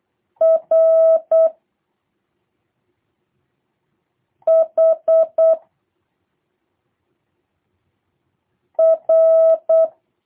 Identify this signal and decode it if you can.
Morse code, heard as a keyed tone, about 8 wpm, 650 Hz. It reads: R H R
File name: jh8cbh / 2023cw / review1-2